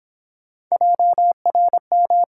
Morse code, heard as a keyed tone, about 30 wpm, 700 Hz. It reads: JRM